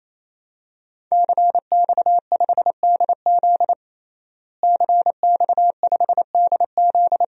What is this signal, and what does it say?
Morse code, heard as a keyed tone, about 28 wpm, 700 Hz. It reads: CX5DZ CX5DZ
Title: DXCC-100 Call Signs - 5 Characters - Repeated 3x - 28wpm